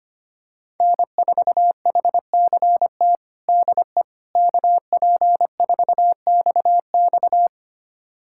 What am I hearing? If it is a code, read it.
Morse code, heard as a keyed tone, about 25 wpm, 700 Hz. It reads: N4HCT DE KP4XX
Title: USA Callsign DE Callsign - 5 Characters - 25wpm